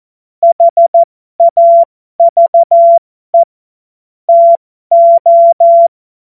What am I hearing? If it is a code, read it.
Morse code, heard as a keyed tone, about 14 wpm, spaced as for 12 wpm, 680 Hz. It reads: HAVE TO
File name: Morsekurs / Witze